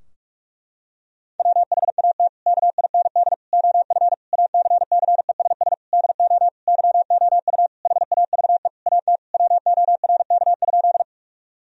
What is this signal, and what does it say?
Morse code, heard as a keyed tone, about 46 wpm, 700 Hz. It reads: WHAT KIND OF ACCESS DO YOU HAVE AT WORK?
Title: Sentences from Top 1000 Words - Review Part 4 - 45wpm